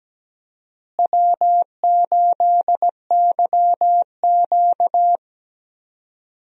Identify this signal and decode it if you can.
Morse code, heard as a keyed tone, about 17 wpm, 700 Hz. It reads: W8YQ
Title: USA Call Signs - 4 Characters - Repeated 3x - 17wpm